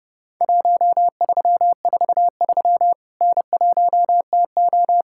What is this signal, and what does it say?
Morse code, heard as a keyed tone, about 30 wpm, 700 Hz. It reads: 1343 N1TO
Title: Worked All Europe DX Contest - QTC Exchange - 30wpm